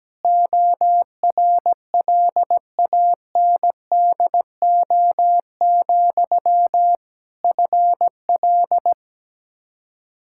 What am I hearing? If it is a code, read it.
Morse code, heard as a keyed tone, about 17 wpm, 700 Hz. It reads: ORLANDO, FL
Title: Practice QSO - Exchange 1 - Beginning - 17wpm